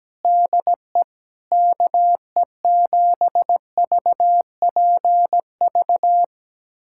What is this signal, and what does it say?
Morse code, heard as a keyed tone, about 17 wpm, 700 Hz. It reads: DE KE7VPV
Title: USA Callsign DE Callsign - 6 Characters - 17wpm